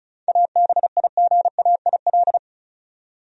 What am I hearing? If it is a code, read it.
Morse code, heard as a keyed tone, about 36 wpm, 700 Hz. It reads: ABIGAIL
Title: USA Common Names - 35wpm